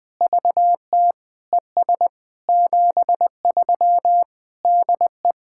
Morse code, heard as a keyed tone, about 20 wpm, 700 Hz. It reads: VT ES 73 DE
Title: POTA QSO Practice - 20wpm